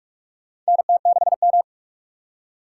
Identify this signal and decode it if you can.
Morse code, heard as a keyed tone, about 46 wpm, 700 Hz. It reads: NT6M
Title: USA Call Signs - 4 Characters - 6x Character Spacing - 45wpm